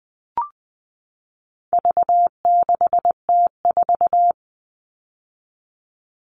Morse code, heard as a keyed tone, about 20 wpm, 700 Hz. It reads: V6T4
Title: All Cut Numbers - 4-Digit 20wpm